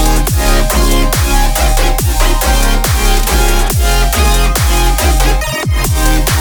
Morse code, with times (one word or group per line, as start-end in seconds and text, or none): none